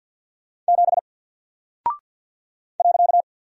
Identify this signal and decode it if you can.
Morse code, heard as a keyed tone, about 48 wpm, 700 Hz. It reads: 7 .
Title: Single Character (Letter:Number:Punctuation:Prosign) Alt 50wpm